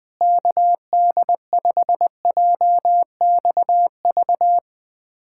Morse code, heard as a keyed tone, about 21 wpm, 700 Hz. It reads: KD5JXV
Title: USA Callsign DE Callsign - 6 Characters - 20wpm